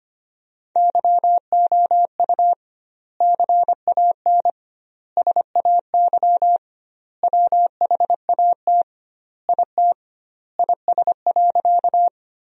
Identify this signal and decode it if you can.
Morse code, heard as a keyed tone, about 25 wpm, 700 Hz. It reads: YOU CAN SAY WHAT IT IS.
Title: 2x Word-Spacing - Sentences from Top 200 Words - 25wpm